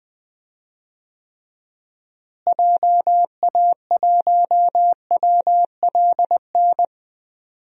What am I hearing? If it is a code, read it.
Morse code, heard as a keyed tone, about 20 wpm, 700 Hz. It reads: JA1WLN
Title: DXCC-100 Call Signs - 6 Characters - Repeated 2x - 20wpm